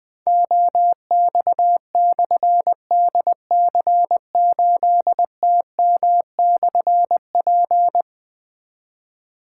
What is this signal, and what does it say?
Morse code, heard as a keyed tone, about 20 wpm, 700 Hz. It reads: OX/DC8TM/P
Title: Difficult Call Signs 20wpm